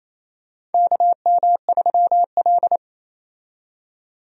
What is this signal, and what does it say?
Morse code, heard as a keyed tone, about 28 wpm, 700 Hz. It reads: KM3L
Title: USA Call Signs - 4 Characters - 28wpm